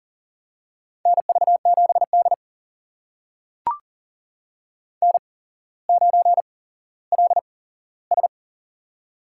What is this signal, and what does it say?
Morse code, heard as a keyed tone, about 42 wpm, spaced as for 17 wpm, 700 Hz. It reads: NV7D N9LS
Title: USA Call Signs - 4 Characters - 8x Character Spacing - 40wpm